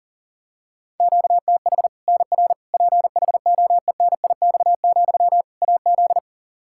Morse code, heard as a keyed tone, about 40 wpm, 700 Hz. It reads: QTH NR PHOENIX, AZ
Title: Practice QSO - Exchange 1 - Beginning - 40wpm